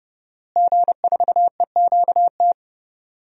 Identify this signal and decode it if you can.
Morse code, heard as a keyed tone, about 31 wpm, 700 Hz. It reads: G4EQT
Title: DXCC-100 Call Signs - 5 Characters - Repeated 2x - 30wpm